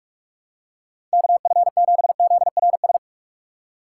Morse code, heard as a keyed tone, about 48 wpm, 700 Hz. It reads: KF7ZRS
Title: USA Call Signs - 6 Characters - 6x Character Spacing - 45wpm